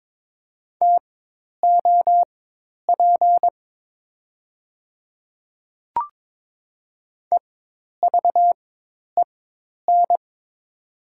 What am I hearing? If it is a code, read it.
Morse code, heard as a keyed tone, about 22 wpm, spaced as for 10 wpm, 700 Hz. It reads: TOP EVEN
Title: 4x Character Spacing - Top 500 Words - 22wpm